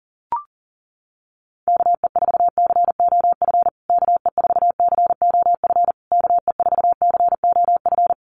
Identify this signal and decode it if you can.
Morse code, heard as a keyed tone, about 40 wpm, 700 Hz. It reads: KE4COF KE4COF KE4COF
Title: USA Call Signs Repeated 3x 40wpm